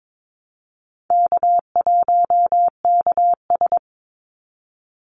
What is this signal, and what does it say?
Morse code, heard as a keyed tone, about 22 wpm, 700 Hz. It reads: K1KS
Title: Easy Call Signs Repeated 3x - 22wpm